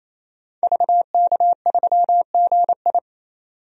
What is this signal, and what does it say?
Morse code, heard as a keyed tone, about 28 wpm, 700 Hz. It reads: VK3GI
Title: DXCC-100 Call Signs - 5 Characters - 28wpm